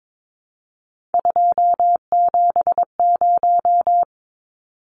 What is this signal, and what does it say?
Morse code, heard as a keyed tone, about 22 wpm, 700 Hz. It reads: 270
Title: Numbers - 3-Digit 22wpm